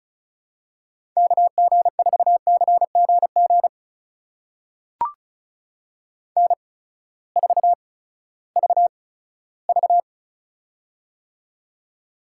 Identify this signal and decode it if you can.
Morse code, heard as a keyed tone, about 36 wpm, spaced as for 18 wpm, 700 Hz. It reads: KG4CGG N4VV
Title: USA Call Signs - 8x Character Spacing - 35wpm